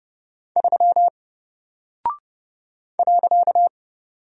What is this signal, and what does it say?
Morse code, heard as a keyed tone, about 30 wpm, 700 Hz. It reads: 3 .
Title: Single Character (Letter:Number:Punctuation:Prosign) Alt 30wpm